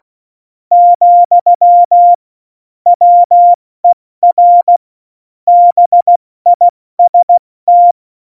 Morse code, heard as a keyed tone, about 16 wpm, 700 Hz. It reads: , WER BIST
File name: Morsekurs / Witze / Witz 03. November 2020, 70 F80